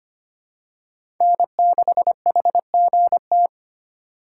Morse code, heard as a keyed tone, about 25 wpm, 700 Hz. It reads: N6HGT
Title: USA Call Signs - 5 Characters - 4x Character Spacing - 25wpm